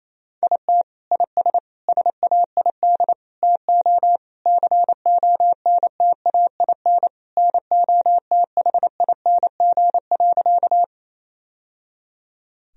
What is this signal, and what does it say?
Morse code, heard as a keyed tone, about 28 wpm, 700 Hz. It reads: IT IS SAID TO CONTAIN NOTHING.
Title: Sentences from Top 1000 Words - Review Part 2 - 28wpm